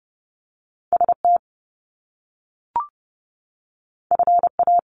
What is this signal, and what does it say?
Morse code, heard as a keyed tone, about 30 wpm, 700 Hz. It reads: ST FA
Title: Top Two Letter Combinations in Words 30wpm